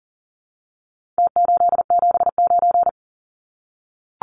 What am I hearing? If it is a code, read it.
Morse code, heard as a keyed tone, about 38 wpm, 700 Hz. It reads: T879
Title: Numbers - Contest Style - 4-Digit 40wpm